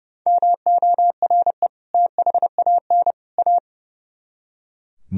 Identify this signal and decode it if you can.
Morse code, heard as a keyed tone, about 35 wpm, 700 Hz. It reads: MORE THAN A
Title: Speed-Racing - Sets of 3 Words 30wpm